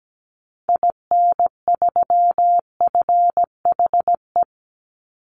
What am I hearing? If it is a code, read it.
Morse code, heard as a keyed tone, about 17 wpm, 700 Hz. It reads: IN3FHE
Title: Call Signs Repeated 2x - 17wpm